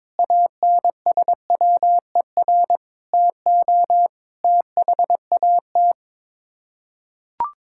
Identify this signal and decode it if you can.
Morse code, heard as a keyed tone, about 22 wpm, 700 Hz. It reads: ANSWER TO THAT
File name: practice / 22wpm / Sets of 7 Words - 22wpm